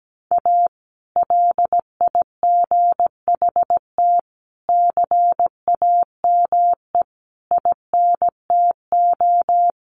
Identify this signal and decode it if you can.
Morse code, heard as a keyed tone, about 17 wpm, 700 Hz. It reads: A LIGHT CAME INTO
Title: Sentences from Top 1100 Words 17wpm